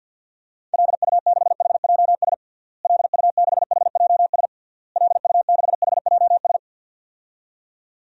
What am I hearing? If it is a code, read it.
Morse code, heard as a keyed tone, about 52 wpm, 700 Hz. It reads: LU6HJS LU6HJS LU6HJS
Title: DXCC-100 Call Signs - 6 Characters - Repeated 3x - 50wpm